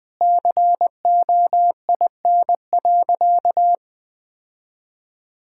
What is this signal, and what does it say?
Morse code, heard as a keyed tone, about 20 wpm, 700 Hz. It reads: COIN.
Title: Sentences with 7-Word Phrases - Spoken Beginning - 20wpm